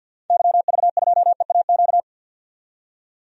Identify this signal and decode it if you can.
Morse code, heard as a keyed tone, about 52 wpm, 700 Hz. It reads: YV2EAQ
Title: DXCC-100 Call Signs - 6 Characters - 50wpm